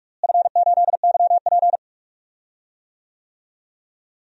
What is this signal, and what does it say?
Morse code, heard as a keyed tone, about 42 wpm, 700 Hz. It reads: F8YP
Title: DXCC-100 Call Signs - 4 Characters - Repeated 2x - 45wpm